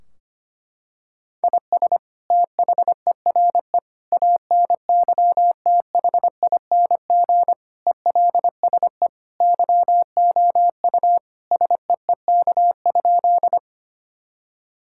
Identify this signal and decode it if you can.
Morse code, heard as a keyed tone, about 25 wpm, 700 Hz. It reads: IS THERE ANYTHING ELSE YOU SEEK?
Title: Sentences from Top 1600 Words 25wpm